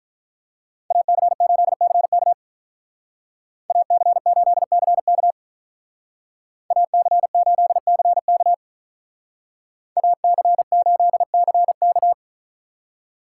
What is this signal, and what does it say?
Morse code, heard as a keyed tone, about 43 wpm, 700 Hz. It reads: AC8CK AC8CK AC8CK AC8CK
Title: Speed Racing - USA Call Signs 35wpm